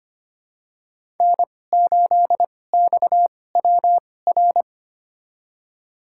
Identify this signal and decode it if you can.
Morse code, heard as a keyed tone, about 25 wpm, spaced as for 18 wpm, 700 Hz. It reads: N8XWR
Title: USA Call Signs - 2x Character Spacing - 25wpm